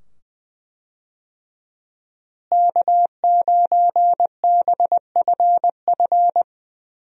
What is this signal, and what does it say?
Morse code, heard as a keyed tone, about 21 wpm, 700 Hz. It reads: K9BFF
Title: USA Call Signs - 5 Characters - Repeated 2x - 20wpm